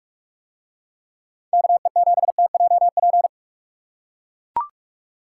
Morse code, heard as a keyed tone, about 44 wpm, 700 Hz. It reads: KE7TJP